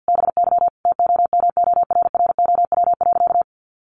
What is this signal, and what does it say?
Morse code, heard as a keyed tone, about 50 wpm, 700 Hz. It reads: BY TOMORROW.
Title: Sentences from Top 1100 Words 50wpm